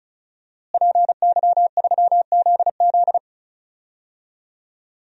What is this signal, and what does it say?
Morse code, heard as a keyed tone, about 34 wpm, 700 Hz. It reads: PY3ZZ
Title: DXCC-100 Call Signs - 5 Characters - Repeated 2x - 35wpm